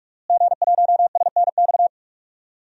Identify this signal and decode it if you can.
Morse code, heard as a keyed tone, about 46 wpm, 700 Hz. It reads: G1SNX
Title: DXCC-100 Call Signs - 5 Characters - Repeated 3x - 45wpm